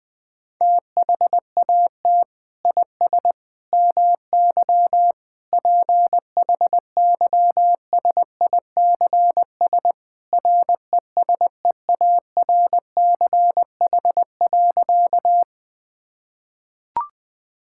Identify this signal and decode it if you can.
Morse code, heard as a keyed tone, about 20 wpm, 700 Hz. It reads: THAT IS MY PHYSICS RESEARCH.
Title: Sentences from Top 1000 Words - Review Part 5 - 20wpm